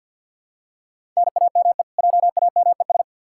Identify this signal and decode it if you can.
Morse code, heard as a keyed tone, about 51 wpm, 700 Hz. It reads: NAME JAMES
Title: Practice QSO - Exchange 1 - Beginning - 50wpm